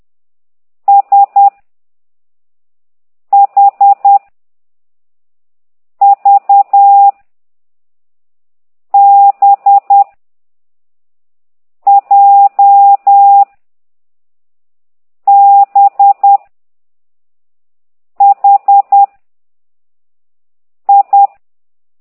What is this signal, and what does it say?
Morse code, heard as a keyed tone, about 10 wpm, 800 Hz. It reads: S H V B J B H I
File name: ct2idw / exerc9